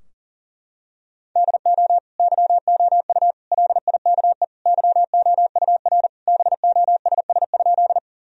Morse code, heard as a keyed tone, about 40 wpm, 700 Hz. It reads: DO YOU LIKE YOUR BOSS?
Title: Sentences from Top 1000 Words - Review Part 4 - 40wpm